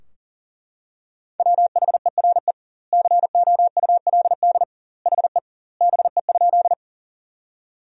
Morse code, heard as a keyed tone, about 40 wpm, 700 Hz. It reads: WHERE COULD HE BE?
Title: 2x Word-Spacing - Sentences from Top 200 Words - 40wpm